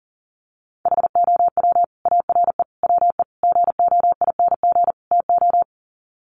Morse code, heard as a keyed tone, about 40 wpm, 700 Hz. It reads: HOW ARE WE GOING TO
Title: Sets of 5 Words Encore 40wpm